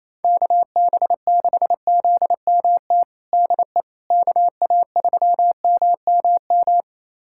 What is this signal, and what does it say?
Morse code, heard as a keyed tone, about 28 wpm, 700 Hz. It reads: KB6ZMT DE KA3MMM
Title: USA Callsign DE Callsign 28wpm